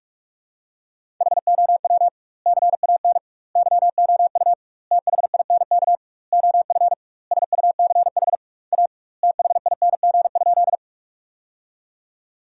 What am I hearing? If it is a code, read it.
Morse code, heard as a keyed tone, about 46 wpm, 700 Hz. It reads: HOW CAN YOU THINK OF SUCH A THING?
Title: 2x Word-Spacing - Sentences from Top 200 Words - 45wpm